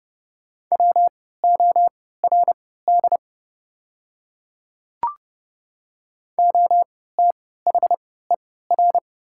Morse code, heard as a keyed tone, about 30 wpm, spaced as for 17 wpm, 700 Hz. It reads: WORD OTHER
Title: 3x Character Spacing - Top 100 Words - 30wpm